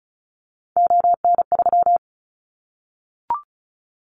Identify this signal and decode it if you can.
Morse code, heard as a keyed tone, about 34 wpm, 700 Hz. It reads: ON3
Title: Callsign Prefixes for Top 25 DXCC - 35wpm